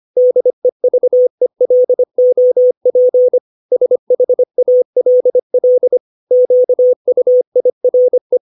Morse code, heard as a keyed tone, about 25 wpm, 500 Hz. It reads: DEVELOP SHALL QUIRE
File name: 632 words / 632 Words 25WPM